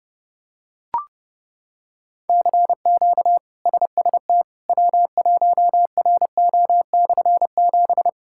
Morse code, heard as a keyed tone, about 30 wpm, 700 Hz. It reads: CQ SST W1RO/7
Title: K1USN SST QSO Practice - 30wpm